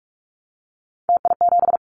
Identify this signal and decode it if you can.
Morse code, heard as a keyed tone, about 44 wpm, 700 Hz. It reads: TI7
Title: Callsign Prefixes for Top 100 DXCC - 45wpm